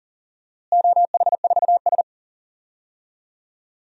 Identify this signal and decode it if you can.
Morse code, heard as a keyed tone, about 40 wpm, 700 Hz. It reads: OH4S
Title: DXCC-100 Call Signs - 4 Characters - Repeated 3x - 40wpm